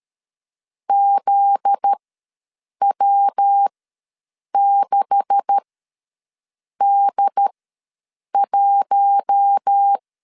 Morse code, heard as a keyed tone, about 13 wpm, spaced as for 7 wpm, 800 Hz. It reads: ZW6D1
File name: LEZIONI / 9 286 ws 12wpm 3351